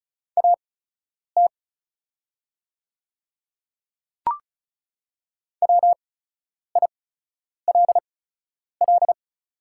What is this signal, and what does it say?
Morse code, heard as a keyed tone, about 36 wpm, spaced as for 10 wpm, 700 Hz. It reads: AT WILL
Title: 8x Character Spacing - Top 100 Words - 35wpm